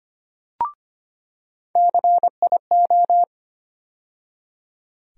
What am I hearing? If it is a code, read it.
Morse code, heard as a keyed tone, about 25 wpm, 700 Hz. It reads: CIO
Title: USA Call Sign Postfixes - 3 Letters 25wpm